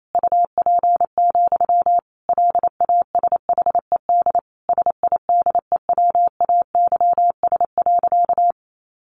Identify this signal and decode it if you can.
Morse code, heard as a keyed tone, about 28 wpm, 700 Hz. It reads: UP, LASHED SIDEWAYS.
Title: A Sound Of Thunder 28wpm